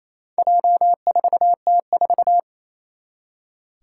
Morse code, heard as a keyed tone, about 27 wpm, 700 Hz. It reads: J4T4
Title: Four Challenging Letter-Number Combinations 28wpm